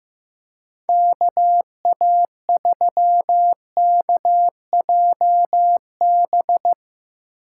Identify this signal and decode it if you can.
Morse code, heard as a keyed tone, about 15 wpm, 700 Hz. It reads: KA3KJB
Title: Speed Racing - USA Call Signs 15wpm